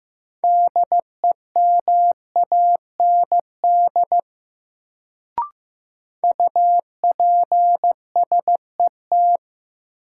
Morse code, heard as a keyed tone, about 15 wpm, 700 Hz. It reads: DEMAND UPSET
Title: Sentences from Top 800 Words - New Word Drill - 15wpm